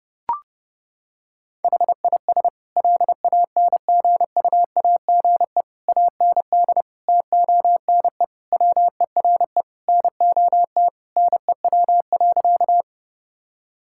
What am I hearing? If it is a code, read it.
Morse code, heard as a keyed tone, about 30 wpm, 700 Hz. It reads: HIS LANGUAGE AND TONE WERE NOT NEW.
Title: Sentences from Top 1600 Words 30wpm